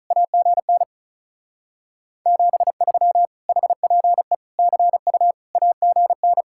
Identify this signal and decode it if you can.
Morse code, heard as a keyed tone, about 43 wpm, 700 Hz. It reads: AGN 73 HPE CU AGN
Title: Speed-Racing Common QSO Phrases - 35wpm